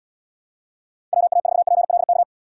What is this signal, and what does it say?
Morse code, heard as a keyed tone, about 71 wpm, 700 Hz. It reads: WA4PLK